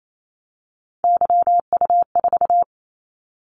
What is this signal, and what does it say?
Morse code, heard as a keyed tone, about 28 wpm, 700 Hz. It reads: YU4
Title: Callsign Prefixes for Top 25 DXCC - 28wpm